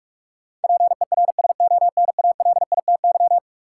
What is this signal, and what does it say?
Morse code, heard as a keyed tone, about 46 wpm, 700 Hz. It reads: PERSONALITY
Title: Sentences from Top 1400 Words - New Word Drill - 45wpm